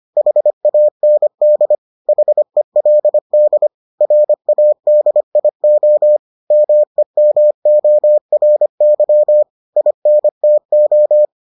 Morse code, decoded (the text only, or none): HAND HELD RADIO MEMORY INTO